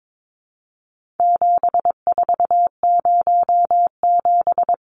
Morse code, heard as a keyed tone, about 22 wpm, 700 Hz. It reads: 7407
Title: Numbers - 4-Digit 22wpm